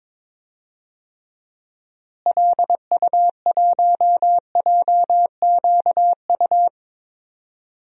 Morse code, heard as a keyed tone, about 22 wpm, 700 Hz. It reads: LU1JQU
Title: DXCC-100 Call Signs - 6 Characters - Repeated 3x - 22wpm